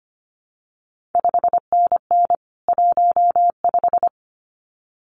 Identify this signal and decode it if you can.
Morse code, heard as a keyed tone, about 25 wpm, 700 Hz. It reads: 5NN 15
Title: CQ WW DX Contest Exchange - 25wpm